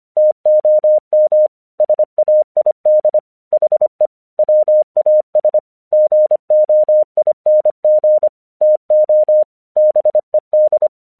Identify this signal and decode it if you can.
Morse code, heard as a keyed tone, about 25 wpm, 610 Hz. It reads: TOM SAID HE WAS GOING TO BED
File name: 1 English 25wpm / AQC18